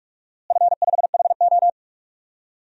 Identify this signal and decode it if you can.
Morse code, heard as a keyed tone, about 46 wpm, 700 Hz. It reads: F5HO
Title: DXCC-100 Call Signs - 4 Characters - Repeated 3x - 45wpm